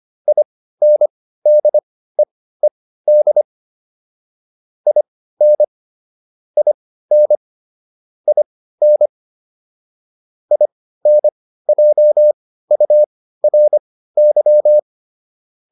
Morse code, heard as a keyed tone, about 25 wpm, spaced as for 15 wpm, 600 Hz. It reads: INDEED IN IN IN INJURY